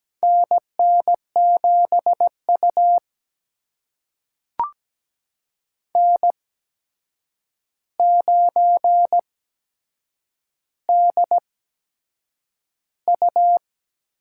Morse code, heard as a keyed tone, about 17 wpm, spaced as for 7 wpm, 700 Hz. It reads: NN7U N9DU
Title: USA Call Signs - 4 Characters - 8x Character Spacing - 17wpm